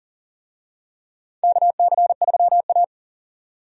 Morse code, heard as a keyed tone, about 40 wpm, 700 Hz. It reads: KC3A
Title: USA Call Signs - 4 Characters - 40wpm